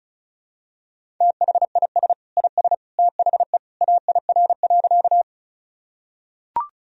Morse code, heard as a keyed tone, about 34 wpm, 700 Hz. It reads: THIS IS THE AIR.
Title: Sentences from Top 1000 Words - Review Part 1 - 35wpm